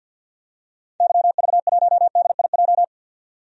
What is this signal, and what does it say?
Morse code, heard as a keyed tone, about 46 wpm, 700 Hz. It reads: YV1DIJ
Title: DXCC-100 Call Signs - 6 Characters - 50wpm